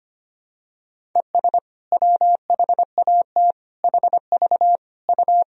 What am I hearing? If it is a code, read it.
Morse code, heard as a keyed tone, about 25 wpm, 700 Hz. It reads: ES WHAT HV U
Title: Sentences from Top 500 Words and Abbreviations 25wpm